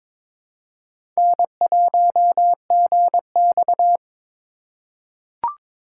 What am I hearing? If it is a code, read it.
Morse code, heard as a keyed tone, about 22 wpm, 700 Hz. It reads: N1GX
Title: USA Call Signs - 4 Characters - 4x Character Spacing - 22wpm